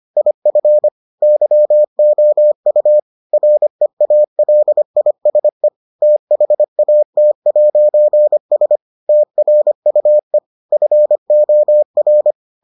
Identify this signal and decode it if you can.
Morse code, heard as a keyed tone, about 25 wpm, 600 Hz. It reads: IF YOU REALISE THAT'S TRUE FOR